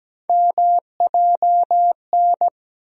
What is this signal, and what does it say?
Morse code, heard as a keyed tone, about 17 wpm, 700 Hz. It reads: MJN
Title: USA Call Sign Postfixes - 3 Letters 17wpm